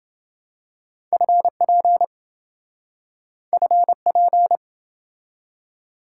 Morse code, heard as a keyed tone, about 27 wpm, 700 Hz. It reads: FP FP